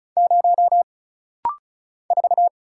Morse code, heard as a keyed tone, about 35 wpm, 700 Hz. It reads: 0 4
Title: Single Letter-Number - Rapid-Fire 35wpm